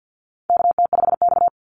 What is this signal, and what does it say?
Morse code, heard as a keyed tone, about 48 wpm, 700 Hz. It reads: KT5X
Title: NA SOTA Chaser Callsigns 50wpm